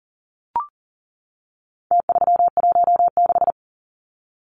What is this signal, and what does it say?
Morse code, heard as a keyed tone, about 40 wpm, 700 Hz. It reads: T316
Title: Numbers - Contest Style - 4-Digit 40wpm